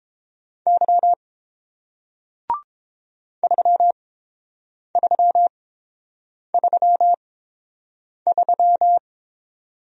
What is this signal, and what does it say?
Morse code, heard as a keyed tone, about 30 wpm, spaced as for 7 wpm, 700 Hz. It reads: Y 3333